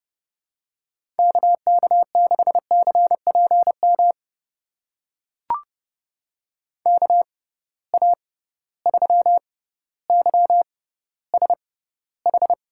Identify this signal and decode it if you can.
Morse code, heard as a keyed tone, about 31 wpm, spaced as for 16 wpm, 700 Hz. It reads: KK6CPM KA3YSH
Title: USA Call Signs - 6 Characters - 6x Character Spacing - 30wpm